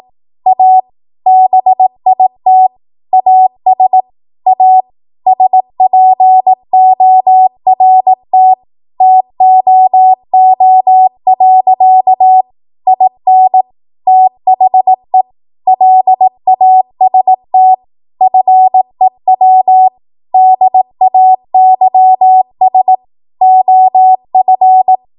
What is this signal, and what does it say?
Morse code, heard as a keyed tone, about 18 wpm, 750 Hz. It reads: A BIT AS A SPORT TOO. IN THE LAST FEW DAYS OF
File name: Morse / 18WPM